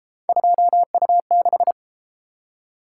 Morse code, heard as a keyed tone, about 32 wpm, 700 Hz. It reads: 2U6